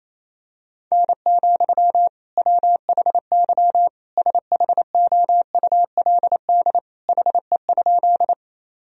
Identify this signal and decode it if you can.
Morse code, heard as a keyed tone, about 28 wpm, 700 Hz. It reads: N, WHY SHOULD HE?
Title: Sentences from Top 300 Words and Abbreviations 28wpm